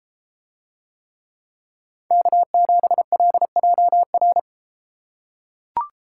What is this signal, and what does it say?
Morse code, heard as a keyed tone, about 32 wpm, 700 Hz. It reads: K7LJR